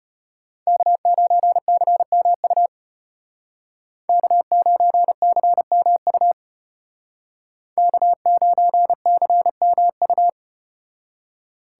Morse code, heard as a keyed tone, about 32 wpm, 700 Hz. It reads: K9CMU K9CMU K9CMU